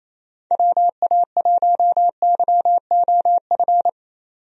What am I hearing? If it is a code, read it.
Morse code, heard as a keyed tone, about 28 wpm, 700 Hz. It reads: WA1YOF
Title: USA Call Signs - 6 Characters - 28wpm